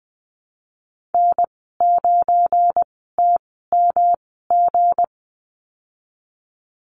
Morse code, heard as a keyed tone, about 20 wpm, spaced as for 15 wpm, 700 Hz. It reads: N9TMG
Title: 173 Lesson 28g - Number 9 - Call Signs 20wpm